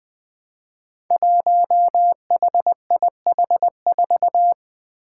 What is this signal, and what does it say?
Morse code, heard as a keyed tone, about 21 wpm, 700 Hz. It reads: 1HIH4
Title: Five Challenging Letter-Number Combinations 20wpm